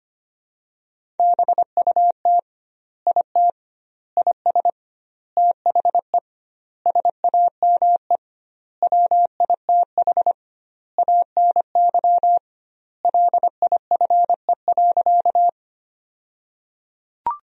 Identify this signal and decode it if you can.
Morse code, heard as a keyed tone, about 25 wpm, 700 Hz. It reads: BUT IT IS THE SAME WITH ANY LIFE.
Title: 2x Word-Spacing - Sentences from Top 300 Words - 25wpm